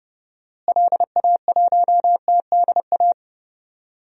Morse code, heard as a keyed tone, about 30 wpm, 700 Hz. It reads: LA1TDA
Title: DXCC-100 Call Signs - 6 Characters - Repeated 2x - 30wpm